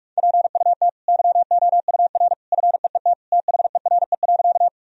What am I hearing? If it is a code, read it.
Morse code, heard as a keyed tone, about 46 wpm, 700 Hz. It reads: PUT YOUR FEET THERE.
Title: Sentences from Top 1000 Words - Review Part 1 - 45wpm